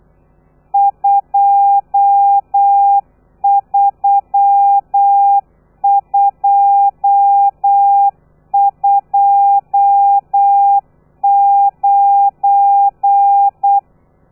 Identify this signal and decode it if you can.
Morse code, heard as a keyed tone, about 8 wpm, 800 Hz. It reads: 23229